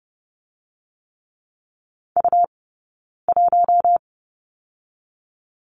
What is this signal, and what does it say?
Morse code, heard as a keyed tone, about 30 wpm, 700 Hz. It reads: U 1
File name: course / 30:10wpm / 099 Lesson 17d - Letter u - Two Characters 30wpm